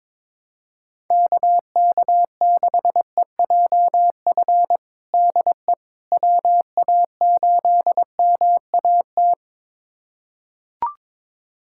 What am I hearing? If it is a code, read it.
Morse code, heard as a keyed tone, about 22 wpm, 700 Hz. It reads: KK6EJF DE WA8MAT
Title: USA Callsign DE Callsign 22wpm